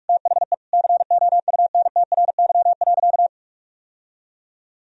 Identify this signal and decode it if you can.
Morse code, heard as a keyed tone, about 46 wpm, 700 Hz. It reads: THE COUNTRY.
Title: Sentences from Top 1000 Words - Review Part 1 - 45wpm